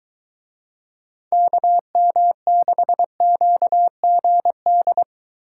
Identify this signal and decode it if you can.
Morse code, heard as a keyed tone, about 23 wpm, 700 Hz. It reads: KM6QGD